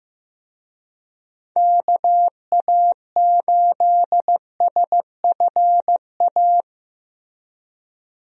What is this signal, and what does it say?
Morse code, heard as a keyed tone, about 15 wpm, 700 Hz. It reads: KA8SFA
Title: USA Call Signs - 6 Characters - Repeated 2x - 15wpm